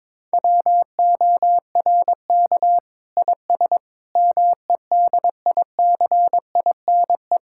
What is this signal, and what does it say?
Morse code, heard as a keyed tone, about 22 wpm, 700 Hz. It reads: WORK IS MEDICINE
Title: Sentences from Top 1000 Words - Review Part 6 - 22wpm